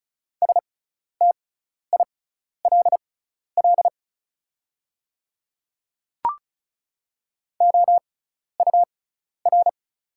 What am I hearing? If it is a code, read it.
Morse code, heard as a keyed tone, about 34 wpm, spaced as for 12 wpm, 700 Hz. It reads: STILL OUR
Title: 6x Character Spacing - Top 300 Words - 35wpm